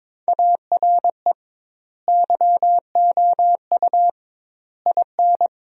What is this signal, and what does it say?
Morse code, heard as a keyed tone, about 22 wpm, 700 Hz. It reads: ARE YOU IN
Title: 2x Word-Spacing - Sentences from Top 400 Words - 22wpm